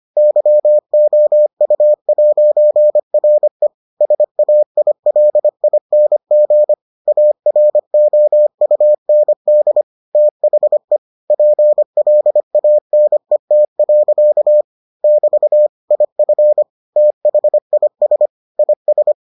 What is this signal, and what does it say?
Morse code, heard as a keyed tone, about 25 wpm, 600 Hz. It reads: YOU'RE SAILING AROUND THE PLANET. <BT> IF THIS IS